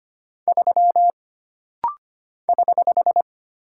Morse code, heard as a keyed tone, about 25 wpm, 700 Hz. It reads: 3 <HH>
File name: Single Character (Letter:Number:Punctuation:Prosign) - Rapid-Fire - 25wpm